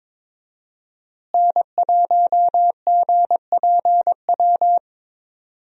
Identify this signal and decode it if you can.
Morse code, heard as a keyed tone, about 21 wpm, 700 Hz. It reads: N1GPW
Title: USA Call Signs - 5 Characters - Repeated 2x - 22wpm